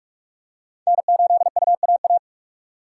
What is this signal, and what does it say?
Morse code, heard as a keyed tone, about 38 wpm, 700 Hz. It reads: N8UAA